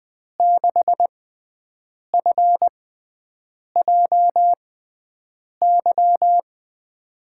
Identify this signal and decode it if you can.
Morse code, heard as a keyed tone, about 20 wpm, spaced as for 7 wpm, 700 Hz. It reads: 6FJY
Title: USA Call Signs - 6x Character Spacing - 20wpm